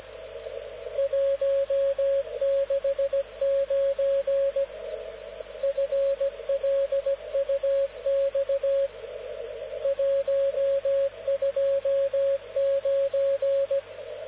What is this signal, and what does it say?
Morse code, heard as a keyed tone, about 17 wpm, 550 Hz, with radio noise. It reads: 169 FLUX 129
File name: luxorion / Radio / dk0wcy-spaceweather-10144khz